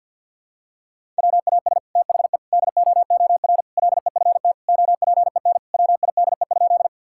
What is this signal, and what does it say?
Morse code, heard as a keyed tone, about 52 wpm, 700 Hz. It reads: WAS THE DOOR LEFT OPEN WIDE?
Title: Sentences from Top 1000 Words - Review Part 6 - 50wpm